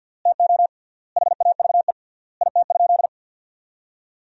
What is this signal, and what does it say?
Morse code, heard as a keyed tone, about 52 wpm, 700 Hz. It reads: TO HAVE IT?
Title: 3x Word-Spacing - Sentences from Top 100 Words - 50wpm